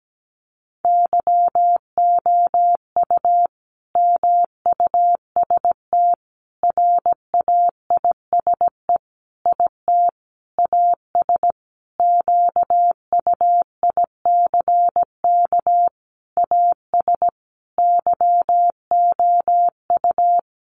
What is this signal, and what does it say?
Morse code, heard as a keyed tone, about 17 wpm, 700 Hz. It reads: YOU MUST RAISE IT AS QUICK AS YOU
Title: Sentences from Top 1400 Words 17wpm